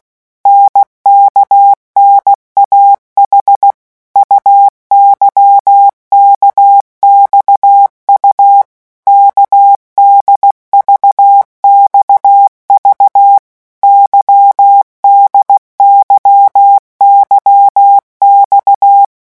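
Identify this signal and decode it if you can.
Morse code, heard as a keyed tone, about 16 wpm, 800 Hz. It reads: NKNAH UYKXU KDVXV YDYYX
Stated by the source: Morse letter-group revision drill at 15 words per minute.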